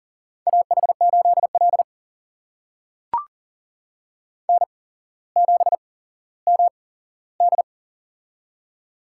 Morse code, heard as a keyed tone, about 40 wpm, spaced as for 17 wpm, 700 Hz. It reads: AH8L N7MD
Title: USA Call Signs - 4 Characters - 8x Character Spacing - 40wpm